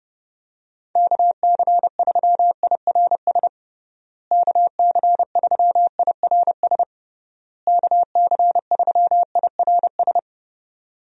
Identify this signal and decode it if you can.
Morse code, heard as a keyed tone, about 32 wpm, 700 Hz. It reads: KC3IRS KC3IRS KC3IRS
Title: USA Call Signs - 6 Characters - Repeated 3x - 30wpm